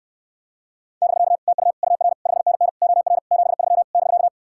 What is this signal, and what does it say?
Morse code, heard as a keyed tone, about 68 wpm, 700 Hz. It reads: <BK> TU UR 5NN OR 73 <BK>